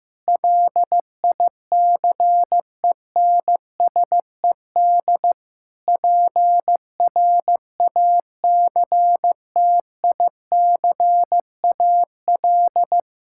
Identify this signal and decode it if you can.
Morse code, heard as a keyed tone, about 15 wpm, 700 Hz. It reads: LICENSED PRACTICAL
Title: Practice QSO - Exchange 3 - More Info - 15wpm